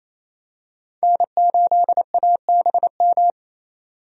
Morse code, heard as a keyed tone, about 28 wpm, 700 Hz. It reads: N8ABM
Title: USA Call Signs - 5 Characters - 28wpm